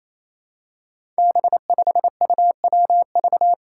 Morse code, heard as a keyed tone, about 28 wpm, 700 Hz. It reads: B5UWV
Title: Five Challenging Letter-Number Combinations 28wpm